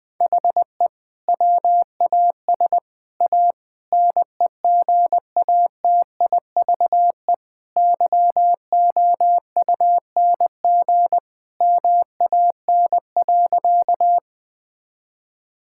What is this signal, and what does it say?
Morse code, heard as a keyed tone, about 20 wpm, 700 Hz. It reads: HE WAS A NEGATIVE YOUNG MAN.
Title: Sentences from Top 1000 Words - Review Part 6 - 20wpm